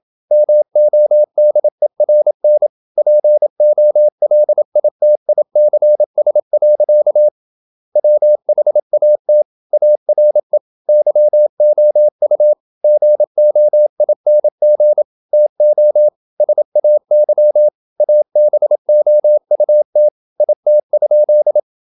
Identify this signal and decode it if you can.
Morse code, heard as a keyed tone, about 27 wpm, 600 Hz. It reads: MODERN POLITICS. WHAT ARE YOU GOING TO SAY ABOUT IT?
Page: Never 26wpm | CW med Gnister